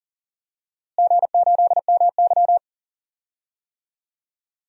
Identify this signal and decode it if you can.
Morse code, heard as a keyed tone, about 40 wpm, 700 Hz. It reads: G8MY
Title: DXCC-100 Call Signs - 4 Characters - Repeated 3x - 40wpm